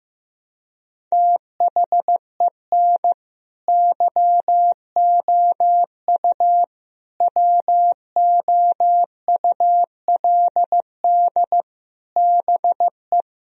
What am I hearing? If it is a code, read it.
Morse code, heard as a keyed tone, about 16 wpm, 700 Hz. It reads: THEN YOU WOULD BE
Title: Sentences from Top 1000 Words - Review Part 1 - 15wpm